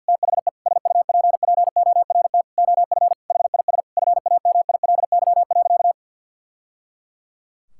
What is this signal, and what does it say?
Morse code, heard as a keyed tone, about 50 wpm, 700 Hz. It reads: THE SUPPORT OF HIS FAMILY.
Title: Sentences from Top 1000 Words - Review Part 5 - 50wpm